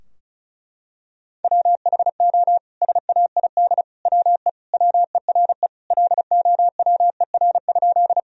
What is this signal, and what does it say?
Morse code, heard as a keyed tone, about 36 wpm, 700 Hz. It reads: WHO SAID WE WERE LOWER?
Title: Sentences from Top 1000 Words - Review Part 4 - 35wpm